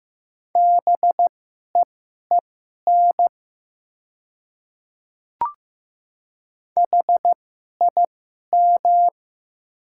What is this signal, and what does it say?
Morse code, heard as a keyed tone, about 15 wpm, spaced as for 11 wpm, 700 Hz. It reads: BEEN HIM
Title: 2x Character Spacing - Top 100 Words - 15wpm